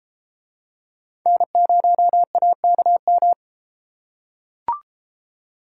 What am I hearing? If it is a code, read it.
Morse code, heard as a keyed tone, about 33 wpm, 700 Hz. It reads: N0AKM